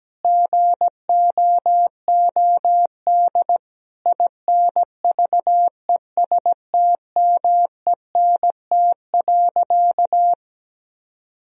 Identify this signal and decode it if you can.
Morse code, heard as a keyed tone, about 17 wpm, 700 Hz. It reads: GOOD INVESTMENT.
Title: Sentences from Top 1000 Words - Review Part 4 - 17wpm